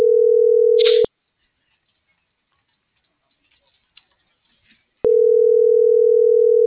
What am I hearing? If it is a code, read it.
Ringback tone (North America), 440 + 480 Hz.